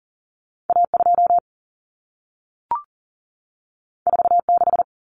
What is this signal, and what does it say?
Morse code, heard as a keyed tone, about 40 wpm, 700 Hz. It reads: A 2 4 6
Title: Common Cut Numbers - 2-Digit 40wpm